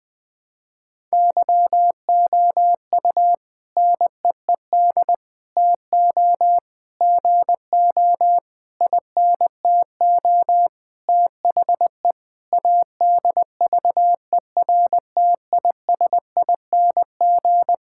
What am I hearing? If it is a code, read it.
Morse code, heard as a keyed tone, about 20 wpm, 700 Hz. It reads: YOU NEED TO GO INTO THE ADVERTISING
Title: Sentences from Top 1000 Words - Review Part 6 - 20wpm